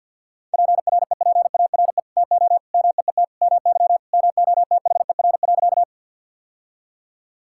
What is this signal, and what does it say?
Morse code, heard as a keyed tone, about 50 wpm, 700 Hz. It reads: PREPARE TO MEET MY MOTHER.
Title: Sentences from Top 1000 Words - Review Part 6 - 50wpm